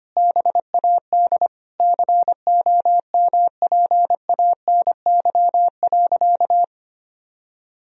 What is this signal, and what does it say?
Morse code, heard as a keyed tone, about 25 wpm, 700 Hz. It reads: BAD COMPANY.
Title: Sentences from Top 1000 Words - Review Part 3 - 25wpm